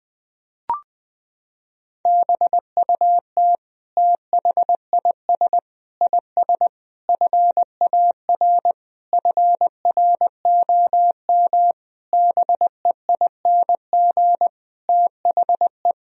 Morse code, heard as a keyed tone, about 20 wpm, 700 Hz. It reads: BUT THIS IS FAR FROM BEING THE
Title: Sentences from Top 1000 Words - Review Part 3 - 20wpm